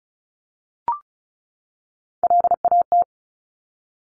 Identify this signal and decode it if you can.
Morse code, heard as a keyed tone, about 36 wpm, 700 Hz. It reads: LAT
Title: Top Three Letter Combinations in Words 35wpm